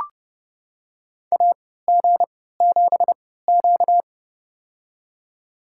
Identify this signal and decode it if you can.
Morse code, heard as a keyed tone, about 30 wpm, spaced as for 17 wpm, 700 Hz. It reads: AG7Q
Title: USA Call Signs - 3x Character Spacing - 30wpm